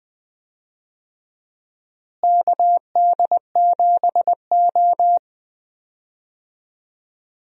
Morse code, heard as a keyed tone, about 20 wpm, 700 Hz. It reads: KD7O
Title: USA Call Signs - 4 Characters - Repeated 3x - 20wpm